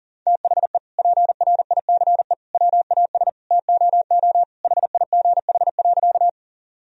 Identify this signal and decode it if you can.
Morse code, heard as a keyed tone, about 37 wpm, 700 Hz. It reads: THE PRICE WAS TOO HIGH.
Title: Sentences from Top 1000 Words - Review Part 4 - 40wpm